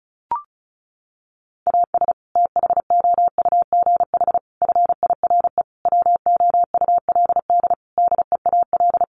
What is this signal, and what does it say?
Morse code, heard as a keyed tone, about 34 wpm, 700 Hz. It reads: AS THOUGH FIRE WOULD DEAL
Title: Sentences from Top 1500 Words 35wpm